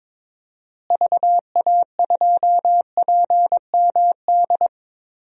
Morse code, heard as a keyed tone, about 22 wpm, 700 Hz. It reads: VA2PMD
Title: DXCC-100 Call Signs - 6 Characters - Repeated 3x - 22wpm